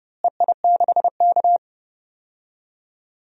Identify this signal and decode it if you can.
Morse code, heard as a keyed tone, about 31 wpm, 700 Hz. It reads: EI6K
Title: DXCC-100 Call Signs - 4 Characters - Repeated 2x - 30wpm